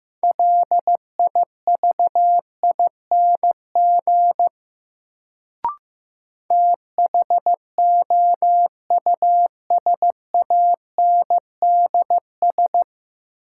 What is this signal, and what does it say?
Morse code, heard as a keyed tone, about 15 wpm, 700 Hz. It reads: LIVING THOUSANDS
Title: Sets of 4 Words - New Word Drill - 15wpm